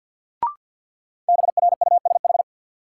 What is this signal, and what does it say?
Morse code, heard as a keyed tone, about 53 wpm, 700 Hz. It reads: BRUSH
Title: Sentences from Top 1100 Words - New Word Drill Alt - 50wpm